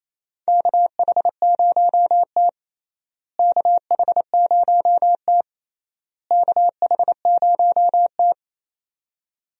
Morse code, heard as a keyed tone, about 28 wpm, 700 Hz. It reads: KH0T KH0T KH0T
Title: USA Call Signs - 4 Characters - Repeated 3x - 28wpm